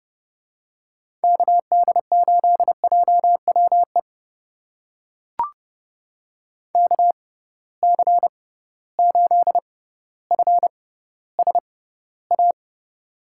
Morse code, heard as a keyed tone, about 30 wpm, spaced as for 15 wpm, 700 Hz. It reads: KD8JWE KC8FSA
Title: USA Call Signs - 6 Characters - 6x Character Spacing - 30wpm